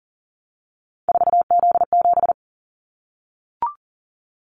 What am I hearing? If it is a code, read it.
Morse code, heard as a keyed tone, about 40 wpm, 700 Hz. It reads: 4Z7
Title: Callsign Prefixes for Top 50 DXCC - 40wpm